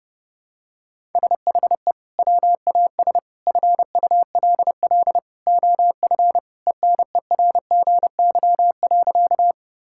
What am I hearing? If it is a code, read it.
Morse code, heard as a keyed tone, about 30 wpm, 700 Hz. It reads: SHE WAS FULL OF ENERGY.
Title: Sentences from Top 1000 Words - Review Part 6 - 30wpm